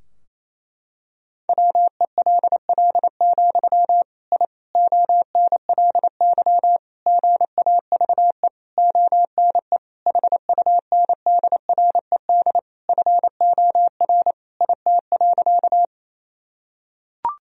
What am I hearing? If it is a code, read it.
Morse code, heard as a keyed tone, about 28 wpm, 700 Hz. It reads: WELL, I ONLY GAVE ONE HUNDRED FOR IT.
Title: Sentences from Top 1000 Words - Review Part 2 - 28wpm